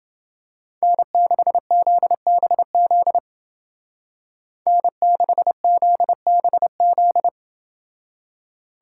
Morse code, heard as a keyed tone, about 29 wpm, 700 Hz. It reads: N6ZBZ N6ZBZ